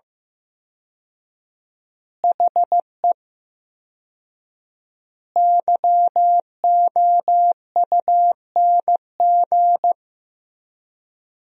Morse code, heard as a keyed tone, about 15 wpm, 700 Hz. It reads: HE YOUNG
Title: 4x Word-Spacing - Sentences from Top 400 Words - 15wpm